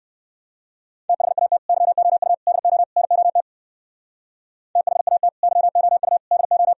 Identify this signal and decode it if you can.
Morse code, heard as a keyed tone, about 58 wpm, 700 Hz. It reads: THAT YOU DO NOT THAT YOU DO